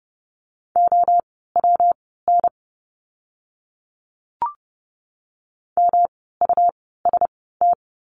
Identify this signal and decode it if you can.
Morse code, heard as a keyed tone, about 30 wpm, spaced as for 17 wpm, 700 Hz. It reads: OWN MUST
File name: course / 30:10wpm / 237 Lesson 38f - Letter q - Words 30wpm